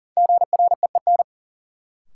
Morse code, heard as a keyed tone, about 40 wpm, 700 Hz. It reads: GREEN
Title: Sentences from Top 500 Words - New Word Drill - 40wpm